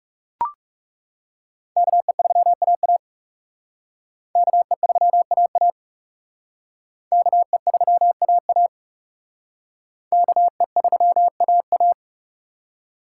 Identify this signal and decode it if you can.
Morse code, heard as a keyed tone, about 38 wpm, 700 Hz. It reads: KE3AA KE3AA KE3AA KE3AA